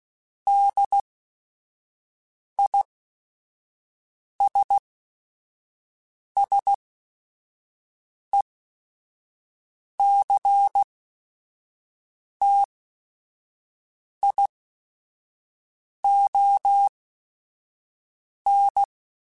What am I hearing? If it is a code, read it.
Morse code, heard as a keyed tone, about 16 wpm, spaced as for 5 wpm, 800 Hz. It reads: DISSECTION
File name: General / General-mp3s / ISS @5